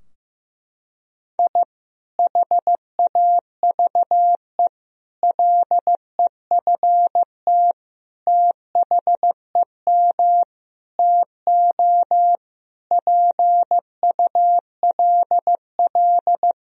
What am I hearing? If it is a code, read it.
Morse code, heard as a keyed tone, about 15 wpm, 700 Hz. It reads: I HAVE LEFT THEM TO PULL
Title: Sentences from Top 1000 Words - Review Part 2 - 15wpm